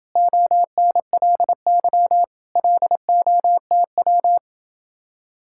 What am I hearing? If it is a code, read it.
Morse code, heard as a keyed tone, about 28 wpm, 700 Hz. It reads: ONLY LOTW